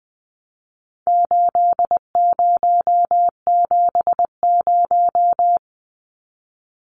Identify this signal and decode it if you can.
Morse code, heard as a keyed tone, about 20 wpm, 700 Hz. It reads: 8070
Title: Numbers - 4-Digit 20wpm